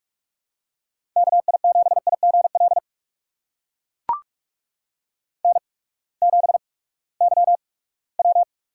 Morse code, heard as a keyed tone, about 44 wpm, spaced as for 22 wpm, 700 Hz. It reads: KI7IGL N7YW
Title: USA Call Signs - 8x Character Spacing - 45wpm